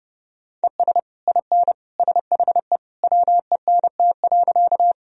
Morse code, heard as a keyed tone, about 32 wpm, 700 Hz. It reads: ES IN SHE WENT.
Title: Sentences from Top 200 Words and Abbreviations 30wpm